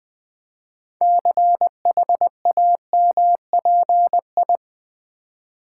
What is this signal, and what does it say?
Morse code, heard as a keyed tone, about 20 wpm, 700 Hz. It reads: CHAMPI
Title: Build A Word - 6 Letters At A Time - 12-Letter Words - 20wpm